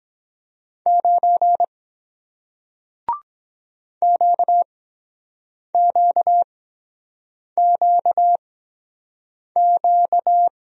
Morse code, heard as a keyed tone, about 23 wpm, spaced as for 7 wpm, 700 Hz. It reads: 9 QQQQ